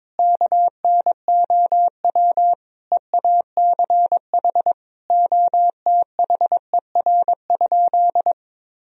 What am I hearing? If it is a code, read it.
Morse code, heard as a keyed tone, about 21 wpm, 700 Hz. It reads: KNOW EACH OTHER?
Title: Sentences from Top 1000 Words - Review Part 3 - 22wpm